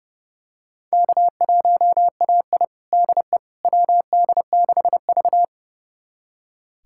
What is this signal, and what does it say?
Morse code, heard as a keyed tone, about 30 wpm, 700 Hz. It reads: K1AI DE WD6V
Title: USA Callsign DE Callsign - 4 Characters - 30wpm